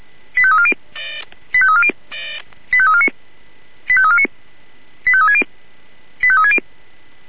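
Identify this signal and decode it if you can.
Five-tone selective call (ZVEI), heard as quick runs of five tones, tones of about 70 ms. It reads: 85340 85340 85349 85349 85360 85360
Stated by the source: ZVEI five-tone selective call received on an SDR, from the Feuerwehr Orlamünde fire brigade audio server.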